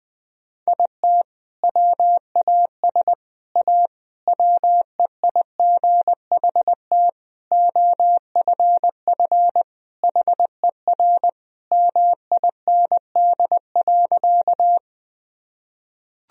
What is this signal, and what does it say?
Morse code, heard as a keyed tone, about 21 wpm, 700 Hz. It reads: IT WAS A WEIGHT OFF HER MIND.
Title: Sentences from Top 1000 Words - Review Part 3 - 20wpm